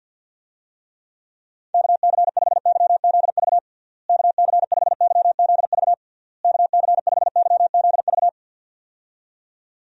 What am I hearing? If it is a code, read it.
Morse code, heard as a keyed tone, about 51 wpm, 700 Hz. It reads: KC5YZV KC5YZV KC5YZV
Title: USA Call Signs - 6 Characters - Repeated 3x - 50wpm